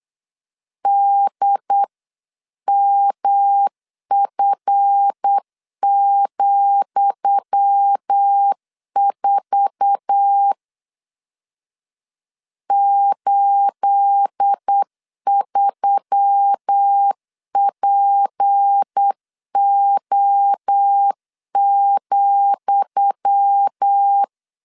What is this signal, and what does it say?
Morse code, heard as a keyed tone, about 8 wpm, 800 Hz. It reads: DMF,4 83PO,
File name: LEZIONI / 10 SIGNS / 10D COMA 8wpm 3749